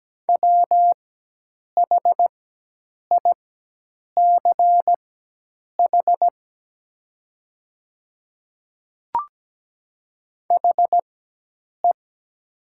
Morse code, heard as a keyed tone, about 17 wpm, spaced as for 8 wpm, 700 Hz. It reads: WHICH HE
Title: 4x Character Spacing - Top 300 Words - 17wpm